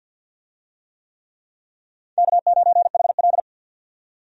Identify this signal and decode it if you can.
Morse code, heard as a keyed tone, about 53 wpm, 700 Hz. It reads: K9HL
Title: USA Call Signs - 4 Characters - 50wpm